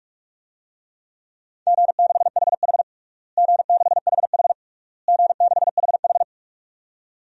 Morse code, heard as a keyed tone, about 41 wpm, 700 Hz. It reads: G6HH G6HH G6HH
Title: DXCC-100 Call Signs - 4 Characters - Repeated 3x - 45wpm